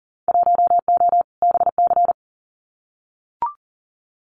Morse code, heard as a keyed tone, about 38 wpm, 700 Hz. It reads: 1O BC
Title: Winter Field Day Exchanges Encore 40wpm